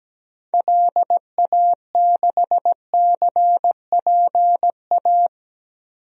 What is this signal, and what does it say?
Morse code, heard as a keyed tone, about 17 wpm, 700 Hz. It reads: LA6CPA
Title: DXCC-100 Call Signs - 6 Characters - Repeated 2x - 17wpm